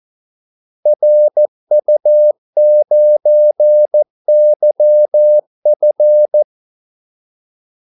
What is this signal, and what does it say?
Morse code, heard as a keyed tone, about 14 wpm, 600 Hz. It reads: RU9YF
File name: Callsigns-14wpm